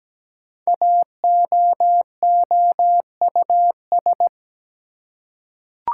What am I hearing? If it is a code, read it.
Morse code, heard as a keyed tone, about 17 wpm, 700 Hz. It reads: AOOUS
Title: Code Groups - 5 Letters - 17wpm